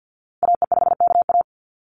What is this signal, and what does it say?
Morse code, heard as a keyed tone, about 50 wpm, 700 Hz. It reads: AE5KA
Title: NA SOTA Activator Callsigns 50wpm